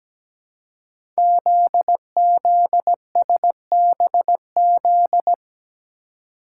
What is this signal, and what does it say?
Morse code, heard as a keyed tone, about 17 wpm, 700 Hz. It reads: ZZSBZ
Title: Code Groups - 5 Letters - 17wpm